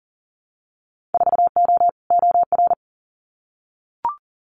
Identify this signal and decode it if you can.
Morse code, heard as a keyed tone, about 38 wpm, 700 Hz. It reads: 4O OR
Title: Winter Field Day Exchanges Encore 40wpm